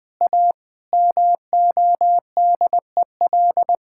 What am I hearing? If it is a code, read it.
Morse code, heard as a keyed tone, about 20 wpm, 700 Hz. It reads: A MODEL
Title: Sentences from Top 1000 Words - Review Part 5 - 20wpm